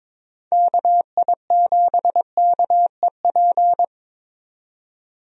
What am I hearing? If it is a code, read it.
Morse code, heard as a keyed tone, about 23 wpm, 700 Hz. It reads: KI7KEP
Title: USA Call Signs - 6 Characters - 22wpm